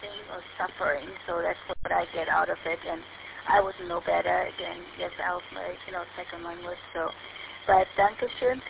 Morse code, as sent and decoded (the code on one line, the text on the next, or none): none